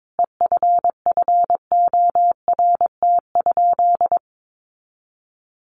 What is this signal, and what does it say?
Morse code, heard as a keyed tone, about 22 wpm, 700 Hz. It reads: EFFORT?
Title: Sentences from Top 1400 Words 22wpm